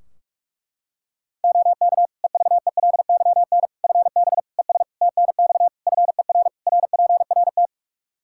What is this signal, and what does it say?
Morse code, heard as a keyed tone, about 46 wpm, 700 Hz. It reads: OK EVELYN FB ES TNX FER RPRT
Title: Practice QSO - Exchange 2 - Info - 45wpm